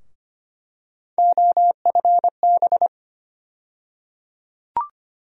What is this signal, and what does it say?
Morse code, heard as a keyed tone, about 25 wpm, 700 Hz. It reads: OFB
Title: USA Call Sign Postfixes - 3 Letters 25wpm